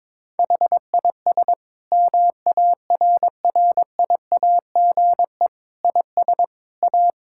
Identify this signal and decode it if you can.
Morse code, heard as a keyed tone, about 22 wpm, 700 Hz. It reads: HIS MARRIAGE IS A
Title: Sentences from Top 1000 Words - Review Part 6 - 22wpm